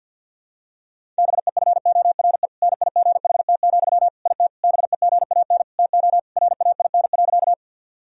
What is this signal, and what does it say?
Morse code, heard as a keyed tone, about 51 wpm, 700 Hz. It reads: BEFORE NIGHT, IT BEGAN TO RAIN.
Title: Sentences from Top 1000 Words - Review Part 1 - 50wpm